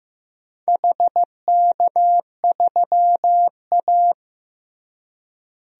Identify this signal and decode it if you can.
Morse code, heard as a keyed tone, about 15 wpm, 700 Hz. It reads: HK3A
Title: DXCC-100 Call Signs - 4 Characters - Repeated 3x - 15wpm